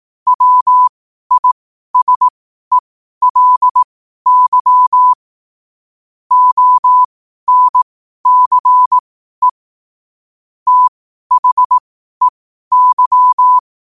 Morse code, heard as a keyed tone, about 18 wpm, spaced as for 12 wpm, 1000 Hz. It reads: WISELY ONCE THEY
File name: qotdfiles / 12wpm